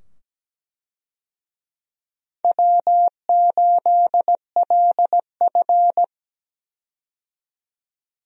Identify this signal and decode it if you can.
Morse code, heard as a keyed tone, about 17 wpm, 700 Hz. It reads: W8LF
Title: USA Call Signs - 4 Characters - 17wpm